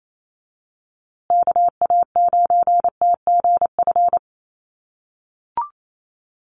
Morse code, heard as a keyed tone, about 28 wpm, 700 Hz. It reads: KA9TGF
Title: USA Call Signs Repeated 2x 28wpm